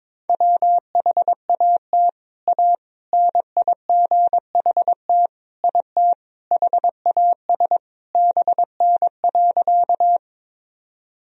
Morse code, heard as a keyed tone, about 22 wpm, 700 Hz. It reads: WHAT A NIGHT IT HAS BN.
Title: Sentences from Top 300 Words and Abbreviations 22wpm